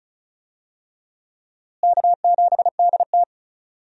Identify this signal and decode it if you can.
Morse code, heard as a keyed tone, about 36 wpm, 700 Hz. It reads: K7DT
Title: USA Call Signs - 4 Characters - Repeated 2x - 35wpm